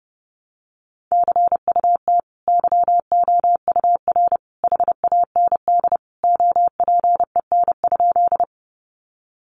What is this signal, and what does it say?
Morse code, heard as a keyed tone, about 30 wpm, 700 Hz. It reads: CUT YOUR HAND OPEN?
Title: Sentences from Top 300 Words 30wpm